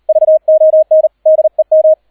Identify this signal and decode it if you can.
Morse code, heard as a keyed tone, about 38 wpm, 630 Hz, with radio noise. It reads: VON DEM